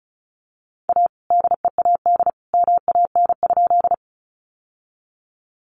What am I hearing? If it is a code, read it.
Morse code, heard as a keyed tone, about 34 wpm, 700 Hz. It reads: A DEAD MAN?
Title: Sentences from Top 1500 Words 35wpm